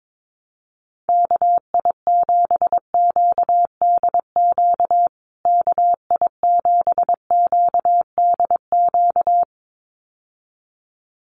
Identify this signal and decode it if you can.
Morse code, heard as a keyed tone, about 22 wpm, 700 Hz. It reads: KI7QDQ KI7QDQ
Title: USA Call Signs Repeated 2x 22wpm